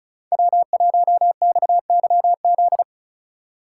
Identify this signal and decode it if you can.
Morse code, heard as a keyed tone, about 34 wpm, 700 Hz. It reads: W1XYZ
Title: USA Call Signs - 5 Characters - 3x Character Spacing - 35wpm